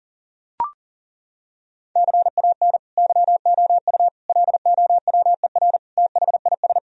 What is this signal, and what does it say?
Morse code, heard as a keyed tone, about 37 wpm, 700 Hz. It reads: CAN YOU LOWER THIS
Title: Sentences from Top 1000 Words - Review Part 4 - 40wpm